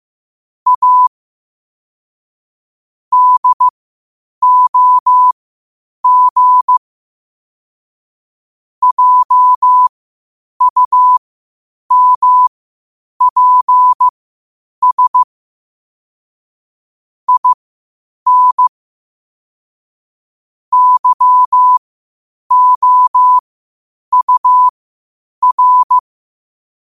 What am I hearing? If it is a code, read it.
Morse code, heard as a keyed tone, about 15 wpm, spaced as for 8 wpm, 1000 Hz. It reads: A DOG JUMPS IN YOUR
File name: qotdfiles / 8wpm